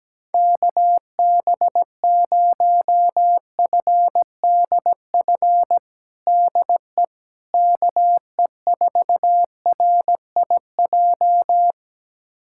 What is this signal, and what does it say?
Morse code, heard as a keyed tone, about 17 wpm, 700 Hz. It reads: KB0FDF DE KE4RIJ
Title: USA Callsign DE Callsign 17wpm